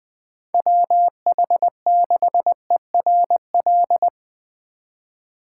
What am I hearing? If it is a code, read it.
Morse code, heard as a keyed tone, about 20 wpm, 700 Hz. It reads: WH6ERL
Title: Speed Racing - USA 2x3 Call Signs 17wpm